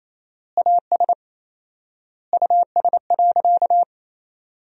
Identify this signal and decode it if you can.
Morse code, heard as a keyed tone, about 28 wpm, 700 Hz. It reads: AS US.
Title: 4x Word-Spacing - Sentences from Top 500 Words - 28wpm